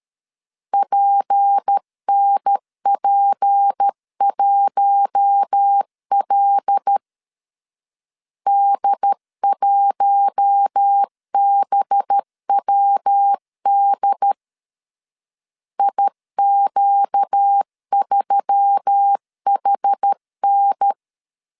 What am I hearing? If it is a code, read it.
Morse code, heard as a keyed tone, about 13 wpm, 800 Hz. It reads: PNP1L D1BWD IQ3HN
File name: CW / DET / 1 6 1500C S C 12wpm 3533